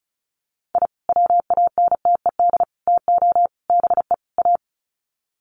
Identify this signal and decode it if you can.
Morse code, heard as a keyed tone, about 34 wpm, 700 Hz. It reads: I WANTED TO BE A
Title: Sets of 5 Words Encore2 35wpm